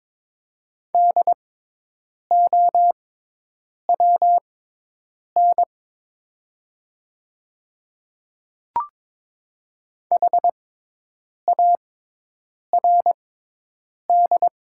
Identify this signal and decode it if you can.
Morse code, heard as a keyed tone, about 22 wpm, spaced as for 8 wpm, 700 Hz. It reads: DOWN HARD
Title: 6x Character Spacing - Top 300 Words - 22wpm